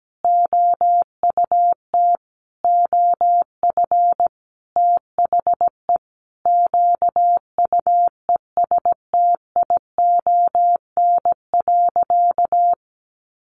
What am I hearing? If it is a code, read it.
Morse code, heard as a keyed tone, about 17 wpm, 700 Hz. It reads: OUT OF THE QUESTION.
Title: Sentences from Top 600 Words 17wpm